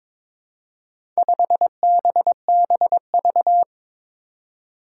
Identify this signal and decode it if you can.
Morse code, heard as a keyed tone, about 22 wpm, 700 Hz. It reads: 5BBV
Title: Four Challenging Letter-Number Combinations 22wpm